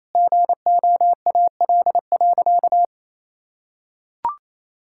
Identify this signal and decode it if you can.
Morse code, heard as a keyed tone, about 28 wpm, 700 Hz. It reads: GOAL.
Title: Sentences from Top 1000 Words - Review Part 6 - 28wpm